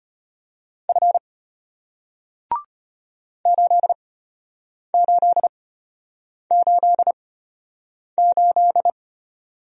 Morse code, heard as a keyed tone, about 33 wpm, spaced as for 8 wpm, 700 Hz. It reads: F 8888